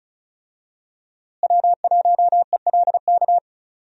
Morse code, heard as a keyed tone, about 36 wpm, 700 Hz. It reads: W1ELK
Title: USA Call Signs - 5 Characters - Repeated 3x - 35wpm